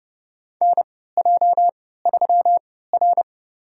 Morse code, heard as a keyed tone, about 31 wpm, spaced as for 17 wpm, 700 Hz. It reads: NJ3R
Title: USA Call Signs - 3x Character Spacing - 30wpm